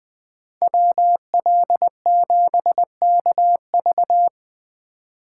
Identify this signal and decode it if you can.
Morse code, heard as a keyed tone, about 24 wpm, 700 Hz. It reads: WL7KV
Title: Speed Racing - USA 2x2 Call Signs 20wpm